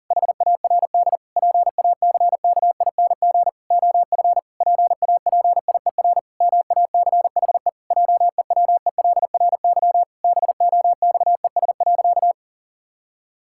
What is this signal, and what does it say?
Morse code, heard as a keyed tone, about 40 wpm, 700 Hz. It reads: HARD PACKING OF PAPIER MACHE JEWELRY BOXES.
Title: Pangram Challenge v2 - 40wpm